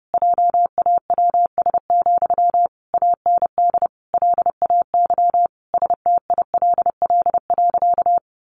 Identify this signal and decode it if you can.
Morse code, heard as a keyed tone, about 30 wpm, 700 Hz. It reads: JAWS, AND LAY STILL.
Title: A Sound Of Thunder 30wpm